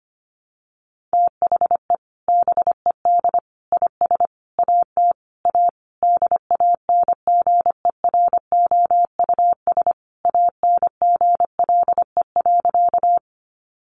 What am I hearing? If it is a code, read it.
Morse code, heard as a keyed tone, about 25 wpm, 700 Hz. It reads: THE BED IS AT A DANGEROUS ANGLE.
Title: Sentences from Top 1500 Words 25wpm